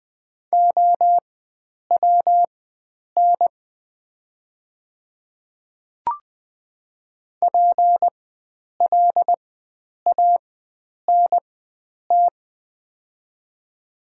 Morse code, heard as a keyed tone, about 20 wpm, spaced as for 9 wpm, 700 Hz. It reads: OWN PLANT
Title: 4x Character Spacing - Top 500 Words - 20wpm